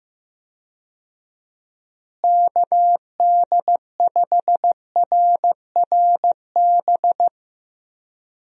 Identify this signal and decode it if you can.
Morse code, heard as a keyed tone, about 15 wpm, 700 Hz. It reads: KD5RRB
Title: USA Call Signs - 6 Characters - Repeated 2x - 15wpm